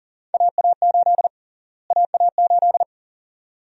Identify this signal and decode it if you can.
Morse code, heard as a keyed tone, about 40 wpm, 700 Hz. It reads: AA8 AA8
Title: USA Call Sign Prefixes - Repeated 2x - 40wpm